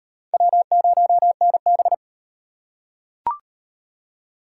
Morse code, heard as a keyed tone, about 35 wpm, 700 Hz. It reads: W0NB